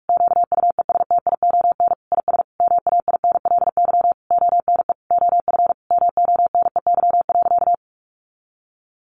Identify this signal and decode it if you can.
Morse code, heard as a keyed tone, about 44 wpm, 700 Hz. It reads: QUESTION IS MAINLY ONE OF MONEY.
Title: Sentences from Top 1400 Words 45wpm